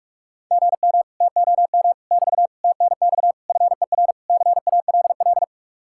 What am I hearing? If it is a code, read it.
Morse code, heard as a keyed tone, about 49 wpm, 700 Hz. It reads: GM TOM <BT> TNX FER CALL